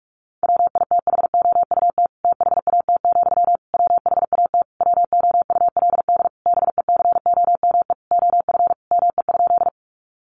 Morse code, heard as a keyed tone, about 44 wpm, 700 Hz. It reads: WITHOUT THAT, WHAT WOULD BECOME OF ME?
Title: Sentences from Top 600 Words 45wpm